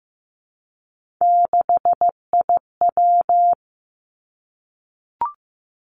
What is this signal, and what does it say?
Morse code, heard as a keyed tone, about 15 wpm, 700 Hz. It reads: 6IW
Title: Three Challenging Letter-Number Combinations 15wpm